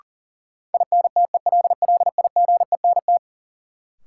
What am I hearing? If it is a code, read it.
Morse code, heard as a keyed tone, about 40 wpm, 700 Hz. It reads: INTELLIGENT
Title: Sentences from Top 1000 Words - Review Part 7 - New Word Drill - 40wpm